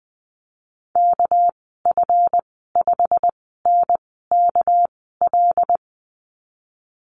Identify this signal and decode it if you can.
Morse code, heard as a keyed tone, about 20 wpm, spaced as for 15 wpm, 700 Hz. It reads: KF5NKL
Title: 219 Lesson 35g - Number 0 - Call Signs 20wpm